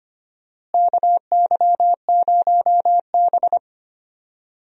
Morse code, heard as a keyed tone, about 25 wpm, 700 Hz. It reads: KY0B
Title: USA Call Signs - 4 Characters - Repeated 3x - 25wpm